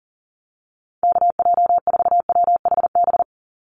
Morse code, heard as a keyed tone, about 40 wpm, 700 Hz. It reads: KJ4WHB
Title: USA Call Signs 40wpm